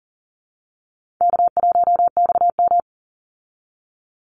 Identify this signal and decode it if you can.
Morse code, heard as a keyed tone, about 40 wpm, 700 Hz. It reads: K1XM
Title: Call Signs Repeated 3x - 40wpm